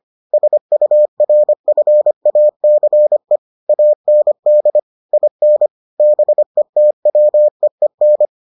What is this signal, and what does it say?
Morse code, heard as a keyed tone, about 25 wpm, 600 Hz. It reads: SURFACE AND IN BETWEEN